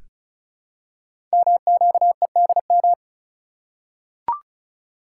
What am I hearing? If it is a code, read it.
Morse code, heard as a keyed tone, about 36 wpm, 700 Hz. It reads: MQEDM
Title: Code Groups - 5 Letter:Numbers - 35wpm